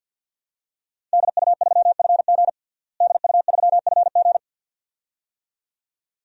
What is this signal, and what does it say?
Morse code, heard as a keyed tone, about 46 wpm, 700 Hz. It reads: DU3FG DU3FG
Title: DXCC-100 Call Signs - 5 Characters - Repeated 2x - 50wpm